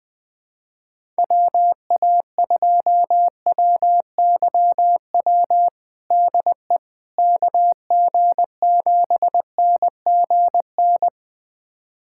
Morse code, heard as a keyed tone, about 20 wpm, 700 Hz. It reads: WA2WYW DE KG7NGN
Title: USA Callsign DE Callsign - 6 Characters - 20wpm